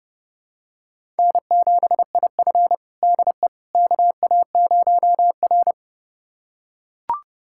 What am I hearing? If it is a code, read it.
Morse code, heard as a keyed tone, about 31 wpm, 700 Hz. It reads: N7IF DE KA0R
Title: USA Callsign DE Callsign - 4 Characters - 30wpm